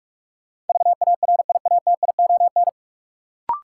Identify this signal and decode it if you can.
Morse code, heard as a keyed tone, about 46 wpm, 700 Hz. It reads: VARIATION
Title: Sentences from Top 1600 Words - New Word Drill - 45wpm